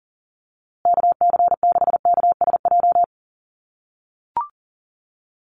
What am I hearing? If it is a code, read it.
Morse code, heard as a keyed tone, about 40 wpm, 700 Hz. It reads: KC6KSJ
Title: USA Call Signs Repeated 3x 40wpm